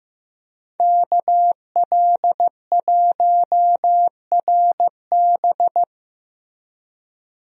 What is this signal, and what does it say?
Morse code, heard as a keyed tone, about 15 wpm, 700 Hz. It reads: KL1RB
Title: USA Call Signs - 5 Characters - Repeated 3x - 15wpm